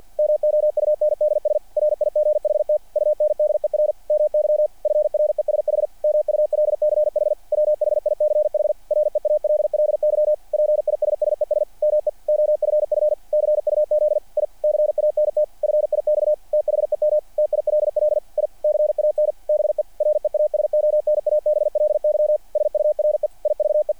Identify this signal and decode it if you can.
Morse code, heard as a keyed tone, about 48 wpm, 600 Hz. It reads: MOUNDS RIGHT UNDER MY FRESH MULCH WHICH REALLY PISSES ME OFF CUZ I CANT FIX THEM TILL I CAN BE REASONABLY SURE IVE
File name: archived / CRSnet